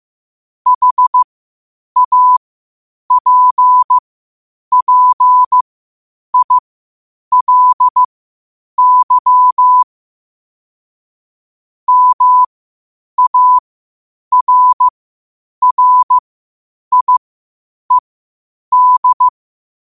Morse code, heard as a keyed tone, about 15 wpm, spaced as for 8 wpm, 1000 Hz. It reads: HAPPILY MARRIED
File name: qotdfiles / 8wpm